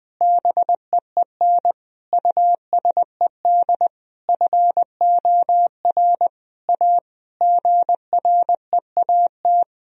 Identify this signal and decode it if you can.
Morse code, heard as a keyed tone, about 20 wpm, 700 Hz. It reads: BEEN USED FOR A GREAT
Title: Sentences from Top 1000 Words - Review Part 1 - 20wpm